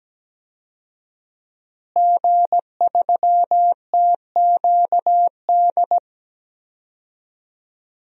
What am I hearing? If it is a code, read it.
Morse code, heard as a keyed tone, about 17 wpm, 700 Hz. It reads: G3TQD
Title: DXCC-100 Call Signs - 5 Characters - Repeated 3x - 17wpm